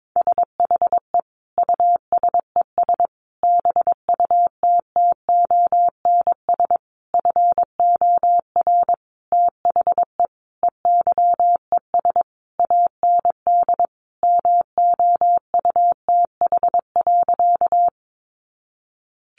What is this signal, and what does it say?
Morse code, heard as a keyed tone, about 22 wpm, 700 Hz. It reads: SHE USES BUTTONS FOR THE EYES AND MOUTH.
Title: Easy Phrases 22wpm